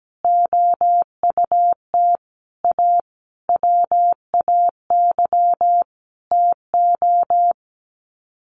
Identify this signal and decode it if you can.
Morse code, heard as a keyed tone, about 17 wpm, 700 Hz. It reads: OUT A WAY TO
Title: Sets of 5 Words Encore2 17wpm